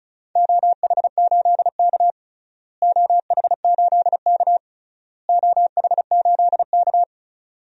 Morse code, heard as a keyed tone, about 36 wpm, 700 Hz. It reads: OH8K OH8K OH8K
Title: DXCC-100 Call Signs - 4 Characters - Repeated 3x - 35wpm